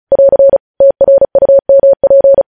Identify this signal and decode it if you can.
Morse code, heard as a keyed tone, about 36 wpm, 550 Hz. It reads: <AR> TRUMP
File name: podcast / News headlines-1x-35x35